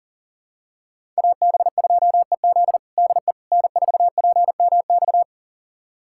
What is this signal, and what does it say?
Morse code, heard as a keyed tone, about 40 wpm, 700 Hz. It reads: AB2EZ DE N4PMX
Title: USA Callsign DE Callsign - 5 Characters - 40wpm